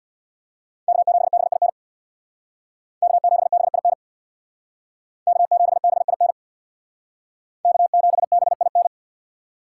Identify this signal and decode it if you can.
Morse code, heard as a keyed tone, about 61 wpm, 700 Hz. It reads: K7BIN K7BIN K7BIN K7BIN
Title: Speed Racing - USA 1x3 Call Signs 50wpm